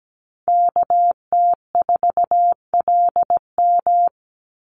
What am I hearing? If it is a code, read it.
Morse code, heard as a keyed tone, about 17 wpm, 700 Hz. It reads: KT4LM
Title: USA Call Signs 17wpm